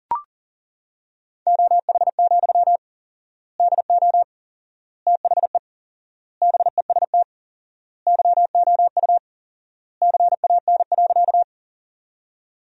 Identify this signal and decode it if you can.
Morse code, heard as a keyed tone, about 40 wpm, 700 Hz. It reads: OH, DO THE BEST YOU CAN.
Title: 4x Word-Spacing - Sentences from Top 500 Words - 40wpm